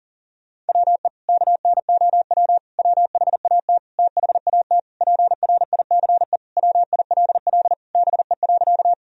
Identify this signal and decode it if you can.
Morse code, heard as a keyed tone, about 40 wpm, 700 Hz. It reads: WE KNOW WHAT THAT PRICE WILL BE.
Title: Sentences from Top 1000 Words - Review Part 4 - 40wpm